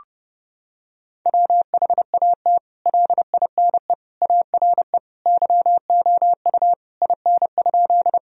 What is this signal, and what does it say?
Morse code, heard as a keyed tone, about 30 wpm, 700 Hz. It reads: WHAT LINE ARE YOU IN?
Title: Sentences from Top 1000 Words - Review Part 1 - 30wpm